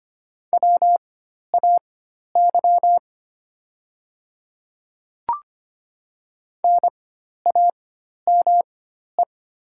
Morse code, heard as a keyed tone, about 25 wpm, spaced as for 12 wpm, 700 Hz. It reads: WAY NAME
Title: 4x Character Spacing - Top 200 Words - 25wpm